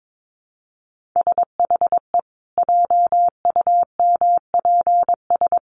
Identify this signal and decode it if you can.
Morse code, heard as a keyed tone, about 22 wpm, 700 Hz. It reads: SHE JUMPS
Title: Easy Phrases 22wpm